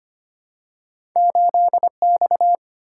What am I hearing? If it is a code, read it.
Morse code, heard as a keyed tone, about 25 wpm, 700 Hz. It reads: 8X
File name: Two Letter-Number Combinations - 25wpm